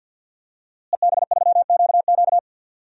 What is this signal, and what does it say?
Morse code, heard as a keyed tone, about 47 wpm, 700 Hz. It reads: EB3QQ
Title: DXCC-100 Call Signs - 5 Characters - Repeated 2x - 50wpm